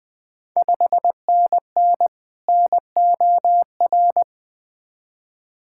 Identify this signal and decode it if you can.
Morse code, heard as a keyed tone, about 20 wpm, 700 Hz. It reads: 5NN NOR
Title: Ontario QSO Party Exchange 20wpm